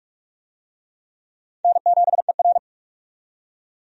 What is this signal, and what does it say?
Morse code, heard as a keyed tone, about 46 wpm, 700 Hz. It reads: N7ER
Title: USA Call Signs - 4 Characters - 45wpm